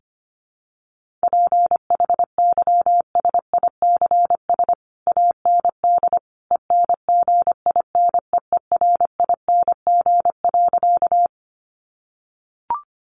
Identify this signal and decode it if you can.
Morse code, heard as a keyed tone, about 25 wpm, 700 Hz. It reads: PHYSICS AND ENGINEERING.
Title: Sentences from Top 1500 Words 25wpm